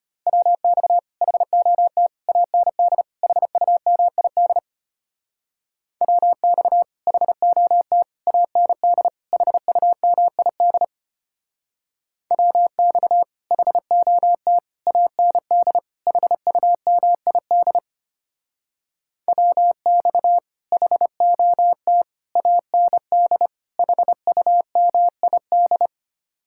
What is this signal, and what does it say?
Morse code, heard as a keyed tone, about 30 wpm, 700 Hz. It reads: WX HOT AND HUMID WX HOT AND HUMID WX HOT AND HUMID WX HOT AND HUMID
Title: Speed-Racing Common QSO Phrases - 25wpm